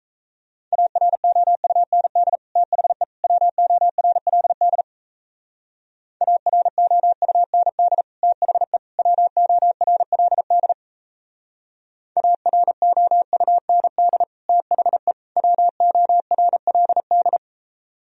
Morse code, heard as a keyed tone, about 37 wpm, 700 Hz. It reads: AROUND THE WORLD AROUND THE WORLD AROUND THE WORLD